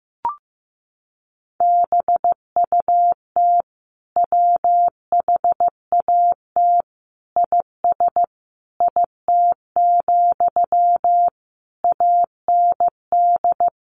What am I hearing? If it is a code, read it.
Morse code, heard as a keyed tone, about 15 wpm, 700 Hz. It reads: BUT WHAT IS IT, AND
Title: Sentences from Top 300 Words 15wpm